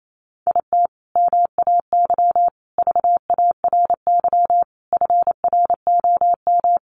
Morse code, heard as a keyed tone, about 28 wpm, 700 Hz. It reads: IT MAY VARY FROM
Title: Sentences from Top 1200 Words 28wpm